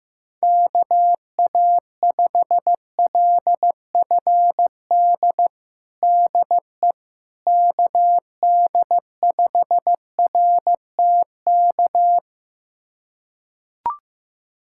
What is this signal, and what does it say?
Morse code, heard as a keyed tone, about 15 wpm, 700 Hz. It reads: KA5LFD DE KD5RTK
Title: USA Callsign DE Callsign - 6 Characters - 15wpm